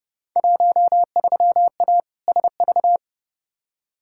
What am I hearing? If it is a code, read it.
Morse code, heard as a keyed tone, about 31 wpm, 700 Hz. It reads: 13A SV
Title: Field Day Exchanges Encore - 30wpm